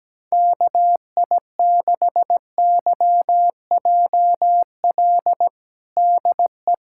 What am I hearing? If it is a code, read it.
Morse code, heard as a keyed tone, about 17 wpm, 700 Hz. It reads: KI6YJL DE
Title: USA Callsign DE Callsign - 6 Characters - 17wpm